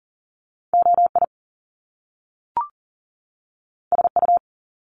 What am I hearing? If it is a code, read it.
Morse code, heard as a keyed tone, about 38 wpm, 700 Hz. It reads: OI SU
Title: Top Two Letter Combinations in Words 40wpm